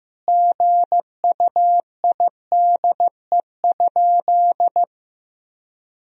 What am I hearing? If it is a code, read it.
Morse code, heard as a keyed tone, about 15 wpm, 700 Hz. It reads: GUIDE?
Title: Sentences from Top 1000 Words - Review Part 3 - 15wpm